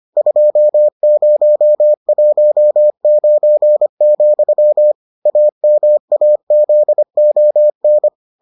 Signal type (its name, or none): Morse code